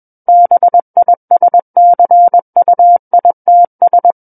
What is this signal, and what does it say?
Morse code, heard as a keyed tone, about 21 wpm, 700 Hz. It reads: BISCUITS
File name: Book One / 12-Chapter 12 - 21 WPM